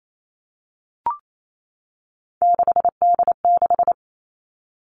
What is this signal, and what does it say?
Morse code, heard as a keyed tone, about 28 wpm, 700 Hz. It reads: 6D6
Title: Three Challenging Letter-Number Combinations 28wpm